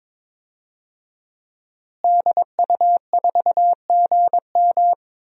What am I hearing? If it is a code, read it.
Morse code, heard as a keyed tone, about 21 wpm, 700 Hz. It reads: DU4GM
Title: DXCC-100 Call Signs - 5 Characters - Repeated 3x - 22wpm